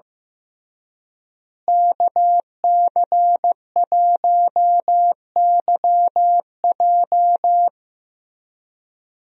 Morse code, heard as a keyed tone, about 16 wpm, 700 Hz. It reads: KC1YJ
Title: USA Call Signs - 5 Characters - Repeated 3x - 15wpm